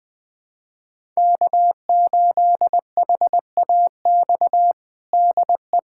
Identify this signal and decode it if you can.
Morse code, heard as a keyed tone, about 21 wpm, 700 Hz. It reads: K8HAX DE
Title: USA Callsign DE Callsign - 5 Characters - 20wpm